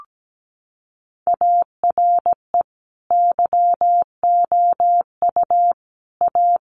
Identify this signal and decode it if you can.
Morse code, heard as a keyed tone, about 17 wpm, 700 Hz. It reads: ARE YOU A
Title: Sentences from Top 1100 Words 17wpm